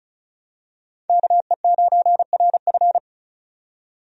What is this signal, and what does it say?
Morse code, heard as a keyed tone, about 36 wpm, 700 Hz. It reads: KE9RF
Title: USA Call Signs - 5 Characters - 35wpm